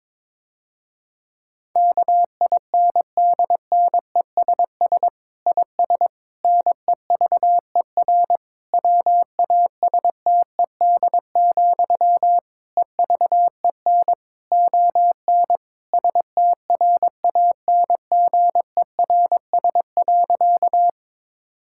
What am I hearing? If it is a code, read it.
Morse code, heard as a keyed tone, about 22 wpm, 700 Hz. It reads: KINDNESS IS NEVER WASTED, EVEN ON STRANGERS.
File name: Difficult Sentences with Hints - 22wpm